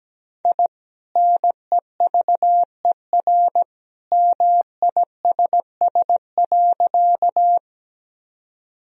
Morse code, heard as a keyed tone, about 17 wpm, 700 Hz. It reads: I NEVER MISS.
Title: Sentences from Top 1000 Words - Review Part 3 - 17wpm